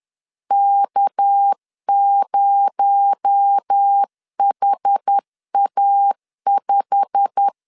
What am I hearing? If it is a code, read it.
Morse code, heard as a keyed tone, about 10 wpm, 800 Hz. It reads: K0HA5
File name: CW / DET / 1 9 1500C S B 10wpm 4439